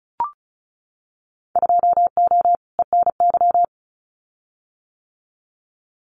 Morse code, heard as a keyed tone, about 36 wpm, 700 Hz. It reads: 2O ENY
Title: Winter Field Day Exchanges 35wpm